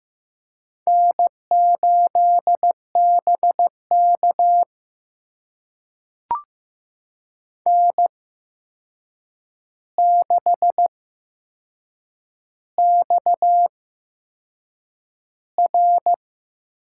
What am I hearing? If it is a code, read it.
Morse code, heard as a keyed tone, about 15 wpm, spaced as for 6 wpm, 700 Hz. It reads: N8BK N6XR
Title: USA Call Signs - 4 Characters - 8x Character Spacing - 15wpm